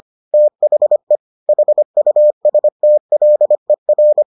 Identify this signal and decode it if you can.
Morse code, heard as a keyed tone, about 25 wpm, 600 Hz. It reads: THE HUSTLER